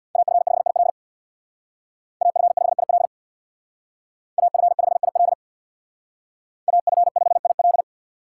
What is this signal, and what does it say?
Morse code, heard as a keyed tone, about 63 wpm, 700 Hz. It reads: AF5IL AF5IL AF5IL AF5IL